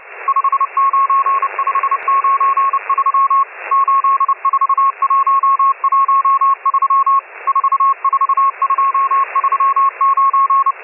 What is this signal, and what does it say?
Morse code, heard as a keyed tone, about 30 wpm, 1090 Hz, with radio noise. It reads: 59593 84113 44239